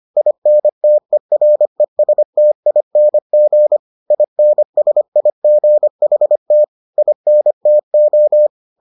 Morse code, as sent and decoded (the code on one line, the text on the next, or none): .. -. - . .-. ... - .. -. --. / .. -. ... .. --. .... - / .. -. - ---
INTERESTING INSIGHT INTO